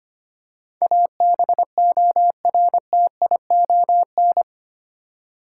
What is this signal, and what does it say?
Morse code, heard as a keyed tone, about 25 wpm, 700 Hz. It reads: ABORTION
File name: 8 Letter Words - First 3 Letters Given - 25wpm